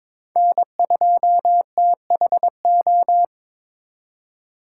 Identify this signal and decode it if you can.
Morse code, heard as a keyed tone, about 26 wpm, 700 Hz. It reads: N2THO
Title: Speed Racing - USA Call Signs 22wpm